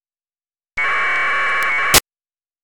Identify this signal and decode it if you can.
Packet radio (AX.25, as in APRS).
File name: amsat / AFSK